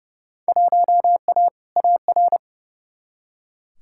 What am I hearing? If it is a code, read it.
Morse code, heard as a keyed tone, about 30 wpm, 700 Hz. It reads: 1A AR
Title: Field Day Exchanges Encore - 30wpm